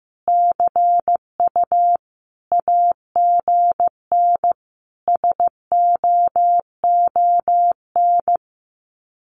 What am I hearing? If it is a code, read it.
Morse code, heard as a keyed tone, about 15 wpm, 700 Hz. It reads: CU AGN SOON
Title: Common QSO Phrases 15wpm